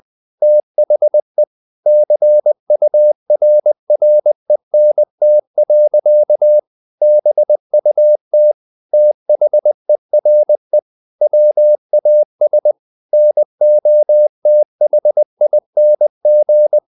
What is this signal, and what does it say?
Morse code, heard as a keyed tone, about 20 wpm, 600 Hz. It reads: THE CURRENT. BUT THERE WAS NOTHING
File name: the old man and the sea part 2 20wpm